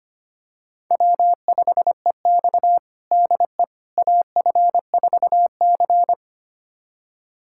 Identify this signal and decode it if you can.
Morse code, heard as a keyed tone, about 25 wpm, 700 Hz. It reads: W5EX DE AF4C
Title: USA Callsign DE Callsign - 4 Characters - 25wpm